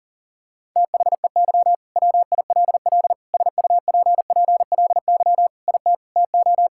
Morse code, heard as a keyed tone, about 37 wpm, 700 Hz. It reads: THEY WILL SUPPLY IT TO
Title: Sentences from Top 1000 Words - Review Part 5 - 40wpm